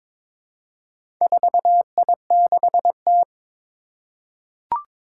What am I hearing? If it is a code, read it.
Morse code, heard as a keyed tone, about 22 wpm, 700 Hz. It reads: 4I6T
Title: Four Challenging Letter-Number Combinations 22wpm